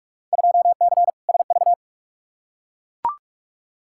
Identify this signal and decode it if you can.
Morse code, heard as a keyed tone, about 42 wpm, 700 Hz. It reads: 2C SV
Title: Field Day Exchanges Encore - 45wpm